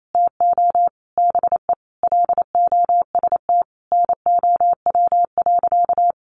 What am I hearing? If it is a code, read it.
Morse code, heard as a keyed tone, about 28 wpm, 700 Hz. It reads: TO BE LOST NOW.
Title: Sentences from Top 1400 Words 28wpm